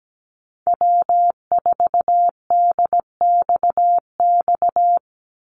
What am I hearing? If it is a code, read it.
Morse code, heard as a keyed tone, about 17 wpm, 700 Hz. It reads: W4DXX
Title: Call Signs Repeated 2x - 17wpm